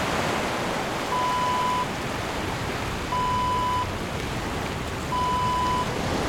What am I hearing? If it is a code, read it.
There is no code.